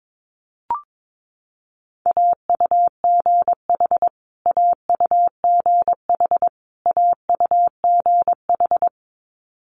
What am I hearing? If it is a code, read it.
Morse code, heard as a keyed tone, about 22 wpm, 700 Hz. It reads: AUGH AUGH AUGH
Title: Multi-letter Phonemes Repeated 3x 22wpm